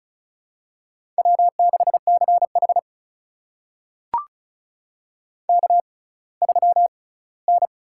Morse code, heard as a keyed tone, about 36 wpm, spaced as for 20 wpm, 700 Hz. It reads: W6CH K3N
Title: USA Call Signs - 4 Characters - 6x Character Spacing - 35wpm